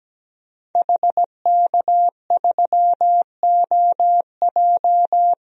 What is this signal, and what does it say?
Morse code, heard as a keyed tone, about 17 wpm, 700 Hz. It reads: HK3OJ
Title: DXCC-100 Call Signs - 5 Characters - 17wpm